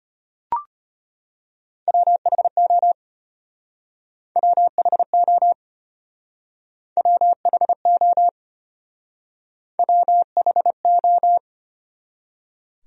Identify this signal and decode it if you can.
Morse code, heard as a keyed tone, about 32 wpm, 700 Hz. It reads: WHO WHO WHO WHO